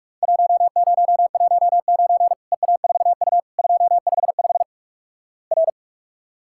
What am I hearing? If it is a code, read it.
Morse code, heard as a keyed tone, about 46 wpm, 700 Hz. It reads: 1019 EA4U 255 R
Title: Worked All Europe DX Contest - QTC Exchange - 45wpm